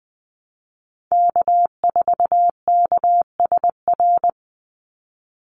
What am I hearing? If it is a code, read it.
Morse code, heard as a keyed tone, about 20 wpm, 700 Hz. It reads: K4KSR
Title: Call Signs Repeated 2x - 20wpm